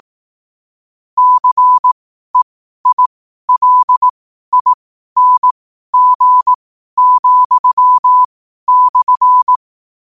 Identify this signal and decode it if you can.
Morse code, heard as a keyed tone, about 18 wpm, spaced as for 12 wpm, 1000 Hz. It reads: CEILING,/
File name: qotdfiles / 12wpm